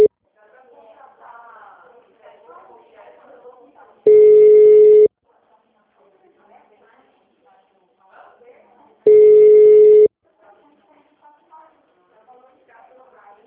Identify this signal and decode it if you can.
Ringback tone (Europe, 425 Hz), with speech.
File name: out- -2001- - - .44196